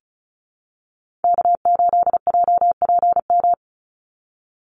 Morse code, heard as a keyed tone, about 34 wpm, 700 Hz. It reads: K8JPM
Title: Call Signs - 35wpm